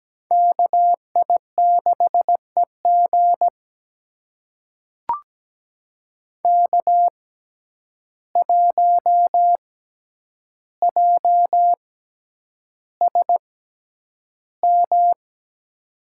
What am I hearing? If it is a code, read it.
Morse code, heard as a keyed tone, about 17 wpm, spaced as for 9 wpm, 700 Hz. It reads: KI6EG K1JSM
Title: USA Call Signs - 5 Characters - 6x Character Spacing - 17wpm